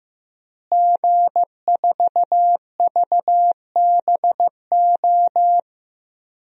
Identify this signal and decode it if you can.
Morse code, heard as a keyed tone, about 15 wpm, 700 Hz. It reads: G4VBO
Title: DXCC-100 Call Signs - 5 Characters - Repeated 3x - 15wpm